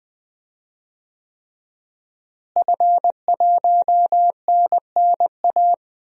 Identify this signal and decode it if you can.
Morse code, heard as a keyed tone, about 20 wpm, 700 Hz. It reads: F1NNA
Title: DXCC-100 Call Signs - 5 Characters - Repeated 3x - 20wpm